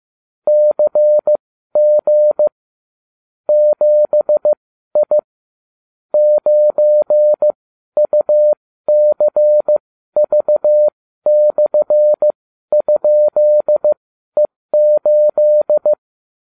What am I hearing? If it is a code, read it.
Morse code, heard as a keyed tone, about 15 wpm, spaced as for 12 wpm, 600 Hz. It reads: CG 7I 9UCV/?E8